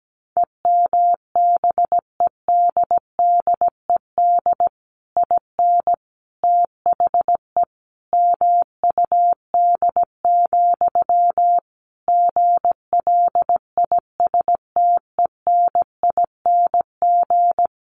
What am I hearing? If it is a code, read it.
Morse code, heard as a keyed tone, about 17 wpm, 700 Hz. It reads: EMBEDDED IN THE MUD, GLISTENING
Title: A Sound Of Thunder 17wpm